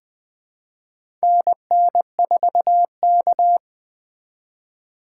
Morse code, heard as a keyed tone, about 20 wpm, 700 Hz. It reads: NN4K
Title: USA Call Signs - 4 Characters - 2x Character Spacing - 20wpm